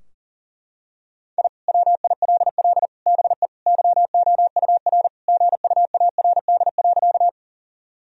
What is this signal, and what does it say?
Morse code, heard as a keyed tone, about 40 wpm, 700 Hz. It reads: I WILL BE YOUR GUARD.
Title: Sentences from Top 1000 Words - Review Part 4 - 40wpm